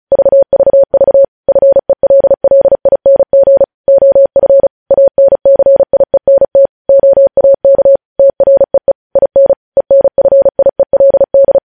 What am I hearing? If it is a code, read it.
Morse code, heard as a keyed tone, about 36 wpm, 550 Hz. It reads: VVV FELLING OF ANCIENT OAK TREE IN ENFIELD
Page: news headlines in morse code at 35 wpm